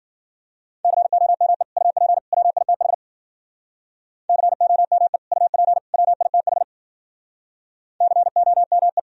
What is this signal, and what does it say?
Morse code, heard as a keyed tone, about 54 wpm, 700 Hz. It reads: COME UP WITH COME UP WITH COME